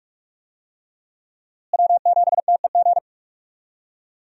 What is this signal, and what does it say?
Morse code, heard as a keyed tone, about 46 wpm, 700 Hz. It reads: W7TEG
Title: USA Call Signs - 5 Characters - Repeated 3x - 45wpm